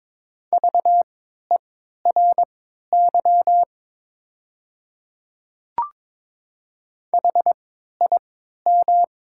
Morse code, heard as a keyed tone, about 22 wpm, spaced as for 13 wpm, 700 Hz. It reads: VERY HIM
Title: 3x Character Spacing - Top 200 Words - 22wpm